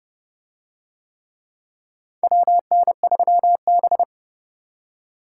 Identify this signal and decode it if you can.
Morse code, heard as a keyed tone, about 25 wpm, 700 Hz. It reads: WN3B